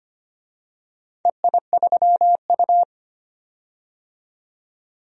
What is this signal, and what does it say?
Morse code, heard as a keyed tone, about 25 wpm, 700 Hz. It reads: EI3U
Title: DXCC-100 Call Signs - 4 Characters - 25wpm